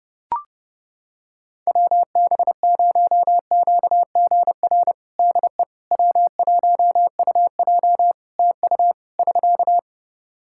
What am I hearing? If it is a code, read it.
Morse code, heard as a keyed tone, about 30 wpm, 700 Hz. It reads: WB0QGR DE W1UJ TU <SK>
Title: Practice QSO - Exchange 4 - Ending - 30wpm